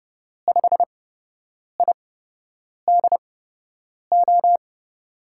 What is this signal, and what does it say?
Morse code, heard as a keyed tone, about 30 wpm, spaced as for 8 wpm, 700 Hz. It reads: 5IDO
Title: USA Call Signs - 6 Characters - 8x Character Spacing - 30wpm